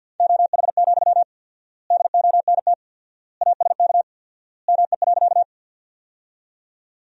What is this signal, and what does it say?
Morse code, heard as a keyed tone, about 50 wpm, 700 Hz. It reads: OH, DONT ASK ME.
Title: 4x Word-Spacing - Sentences from Top 500 Words - 50wpm